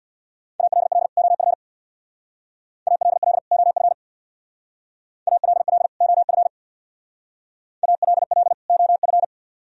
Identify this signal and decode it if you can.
Morse code, heard as a keyed tone, about 63 wpm, 700 Hz. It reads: ALL OF ALL OF ALL OF ALL OF